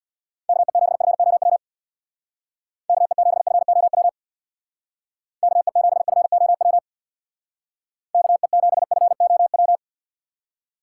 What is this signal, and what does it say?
Morse code, heard as a keyed tone, about 60 wpm, 700 Hz. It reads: KE7FOW KE7FOW KE7FOW KE7FOW
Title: Speed Racing - USA 2x3 Call Signs 50wpm